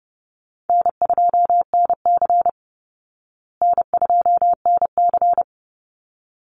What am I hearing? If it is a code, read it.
Morse code, heard as a keyed tone, about 30 wpm, 700 Hz. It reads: N2NC N2NC
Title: Call Signs Repeated 3x - 30wpm